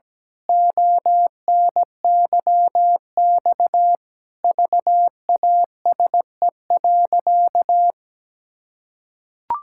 Morse code, heard as a keyed tone, about 17 wpm, 700 Hz. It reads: ONYX VASE.
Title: Pangram Challenge v2 - 17wpm